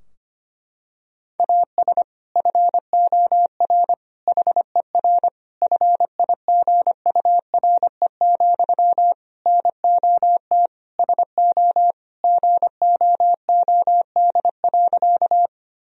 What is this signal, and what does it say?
Morse code, heard as a keyed tone, about 25 wpm, 700 Hz. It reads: AS FOR HER FIGURE, NOT SO GOOD.
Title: Sentences from Top 1000 Words - Review Part 2 - 25wpm